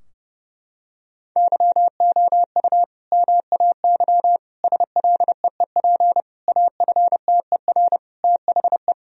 Morse code, heard as a keyed tone, about 30 wpm, 700 Hz. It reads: YOU MAY SLEEP AFTER THE
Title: Sentences from Top 1000 Words - Review Part 4 - 30wpm